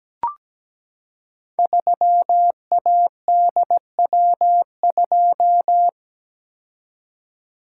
Morse code, heard as a keyed tone, about 17 wpm, 700 Hz. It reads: 3ADW2
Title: Code Groups - 5 Letter:Numbers - 17wpm